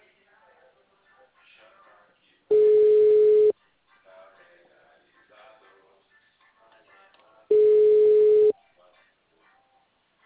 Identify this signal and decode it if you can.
Ringback tone (Europe, 425 Hz), with speech.
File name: out- -2011- - - .1460